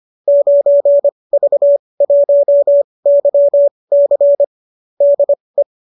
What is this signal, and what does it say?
Morse code, heard as a keyed tone, about 25 wpm, spaced as for 20 wpm, 590 Hz. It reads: 9V1YC DE
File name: qso207 20